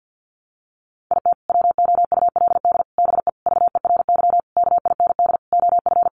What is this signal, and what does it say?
Morse code, heard as a keyed tone, about 50 wpm, 700 Hz. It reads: IT WOULD BE VERY KIND OF
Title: Sentences from Top 200 Words 50wpm